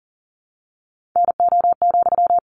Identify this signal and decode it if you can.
Morse code, heard as a keyed tone, about 40 wpm, 700 Hz. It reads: NO,
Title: Sentences from Top 600 Words 40wpm